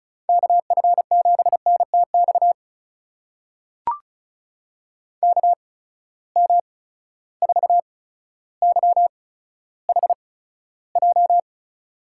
Morse code, heard as a keyed tone, about 34 wpm, spaced as for 15 wpm, 700 Hz. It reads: KF7NTX KM4YHJ
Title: USA Call Signs - 8x Character Spacing - 35wpm